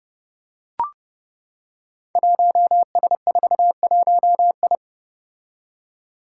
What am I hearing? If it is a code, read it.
Morse code, heard as a keyed tone, about 28 wpm, 700 Hz. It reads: 1S41I
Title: Five Challenging Letter-Number Combinations 30wpm